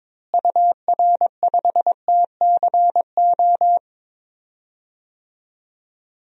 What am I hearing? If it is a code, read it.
Morse code, heard as a keyed tone, about 22 wpm, 700 Hz. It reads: UR5TCO
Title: DXCC-100 Call Signs - 6 Characters - Repeated 2x - 22wpm